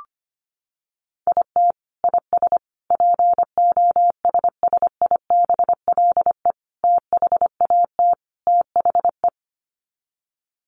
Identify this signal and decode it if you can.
Morse code, heard as a keyed tone, about 25 wpm, 700 Hz. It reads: IT IS POSSIBLE THAT THE
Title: Sets of 5 Words Encore2 25wpm